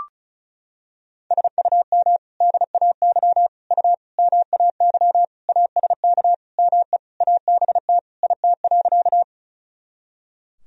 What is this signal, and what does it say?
Morse code, heard as a keyed tone, about 34 wpm, 700 Hz. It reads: SUM DAY U MAY ASK ME ABT IT.
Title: Sentences from Top 200 Words and Abbreviations 35wpm